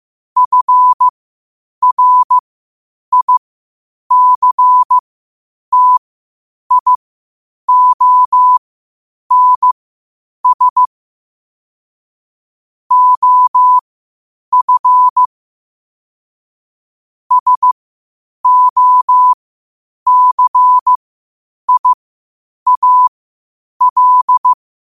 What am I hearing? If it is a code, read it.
Morse code, heard as a keyed tone, about 15 wpm, spaced as for 8 wpm, 1000 Hz. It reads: FRICTIONS OF SOCIAL
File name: qotdfiles / 8wpm